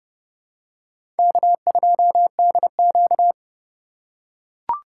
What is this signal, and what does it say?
Morse code, heard as a keyed tone, about 31 wpm, 700 Hz. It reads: K2DQ
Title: USA Call Signs - 4 Characters - 4x Character Spacing - 30wpm